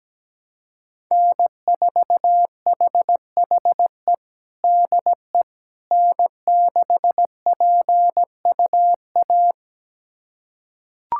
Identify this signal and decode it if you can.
Morse code, heard as a keyed tone, about 17 wpm, 700 Hz. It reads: N4HHE DE N6PUA
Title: USA Callsign DE Callsign - 5 Characters - 17wpm